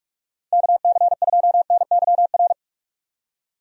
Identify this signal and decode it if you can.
Morse code, heard as a keyed tone, about 46 wpm, 700 Hz. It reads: KC2NYR
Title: USA Call Signs - 6 Characters - 45wpm